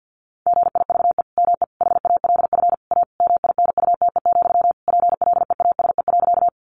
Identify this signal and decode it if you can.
Morse code, heard as a keyed tone, about 50 wpm, 700 Hz. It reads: GIVE ME HALF A MINUTE, PLEASE.
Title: Sentences from Top 1500 Words 50wpm